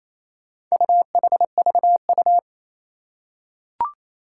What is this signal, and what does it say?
Morse code, heard as a keyed tone, about 28 wpm, 700 Hz. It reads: UHVU
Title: Four Challenging Letter-Number Combinations 28wpm